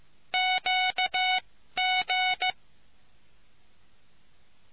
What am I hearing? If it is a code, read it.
Morse code, heard as a keyed tone, about 15 wpm, spaced as for 12 wpm, 740 Hz. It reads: QG